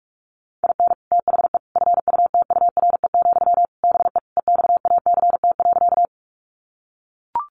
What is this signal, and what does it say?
Morse code, heard as a keyed tone, about 44 wpm, 700 Hz. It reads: IN THE FUTURE, BE EXACT.
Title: Sentences from Top 1300 Words 45wpm